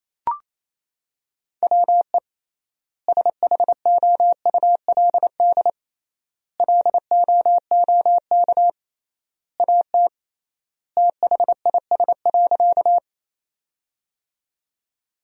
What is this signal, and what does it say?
Morse code, heard as a keyed tone, about 27 wpm, 700 Hz. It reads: WE SHOULD LOOK AT THIS.
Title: 3x Word-Spacing - Sentences from Top 300 Words - 28wpm